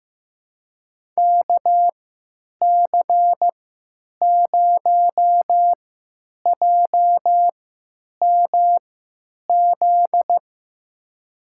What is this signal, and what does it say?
Morse code, heard as a keyed tone, about 15 wpm, spaced as for 9 wpm, 700 Hz. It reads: KC0JMZ
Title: USA Call Signs - 3x Character Spacing - 15wpm